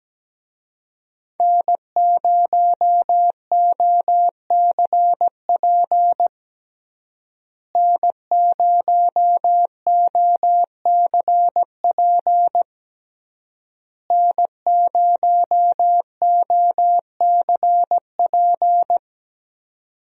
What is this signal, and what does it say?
Morse code, heard as a keyed tone, about 17 wpm, 700 Hz. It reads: N0OCP N0OCP N0OCP
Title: USA Call Signs - 5 Characters - Repeated 3x - 17wpm